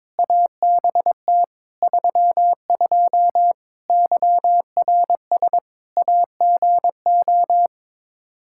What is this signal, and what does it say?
Morse code, heard as a keyed tone, about 21 wpm, 700 Hz. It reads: ABT 32 YRS AGO
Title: Practice QSO - Exchange 3 - More Info - 22wpm